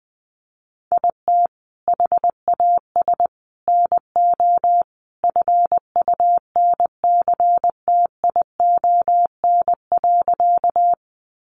Morse code, heard as a keyed tone, about 20 wpm, 700 Hz. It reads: IT HAS NO FUNCTION.
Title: Sentences from Top 1100 Words 20wpm